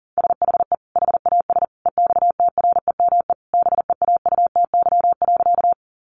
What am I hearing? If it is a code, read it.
Morse code, heard as a keyed tone, about 40 wpm, 700 Hz. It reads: SHE HAS EXTREME BEAUTY.
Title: Sentences from Top 1300 Words 40wpm